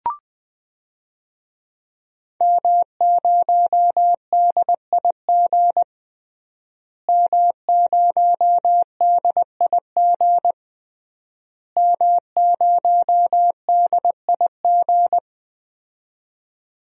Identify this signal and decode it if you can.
Morse code, heard as a keyed tone, about 20 wpm, 700 Hz. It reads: M0DIG M0DIG M0DIG
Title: DXCC-100 Call Signs - 5 Characters - Repeated 3x - 20wpm